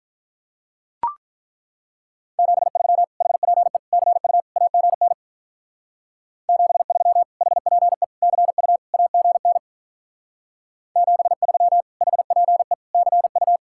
Morse code, heard as a keyed tone, about 43 wpm, 700 Hz. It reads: 73 HPE CU AGN 73 HPE CU AGN 73 HPE CU
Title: Speed-Racing Common QSO Phrases - 35wpm